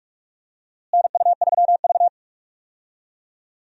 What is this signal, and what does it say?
Morse code, heard as a keyed tone, about 46 wpm, 700 Hz. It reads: NU3V
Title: USA Call Signs - 4 Characters - Repeated 3x - 45wpm